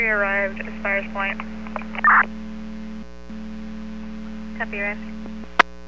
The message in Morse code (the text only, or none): none